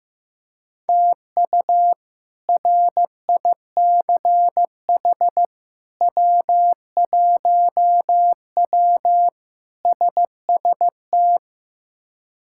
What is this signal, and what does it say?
Morse code, heard as a keyed tone, about 15 wpm, 700 Hz. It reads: TU RICH W1W SST
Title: K1USN SST QSO Practice - 15wpm